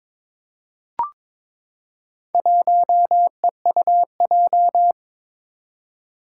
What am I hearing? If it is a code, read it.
Morse code, heard as a keyed tone, about 22 wpm, 700 Hz. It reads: 1EUJ
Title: Four Challenging Letter-Number Combinations 22wpm